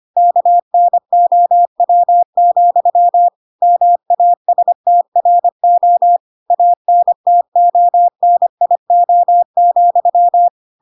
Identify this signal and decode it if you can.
Morse code, heard as a keyed tone, about 25 wpm, 700 Hz. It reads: KNOW, MASTRO ANTONIO,
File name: Pinocchio / Pinocchio ch2 25 25 ik1hge